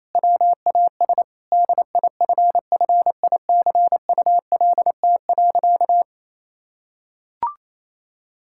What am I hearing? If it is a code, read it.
Morse code, heard as a keyed tone, about 28 wpm, 700 Hz. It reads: WAS DIFFICULT.
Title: Sentences from Top 1000 Words - Review Part 5 - 28wpm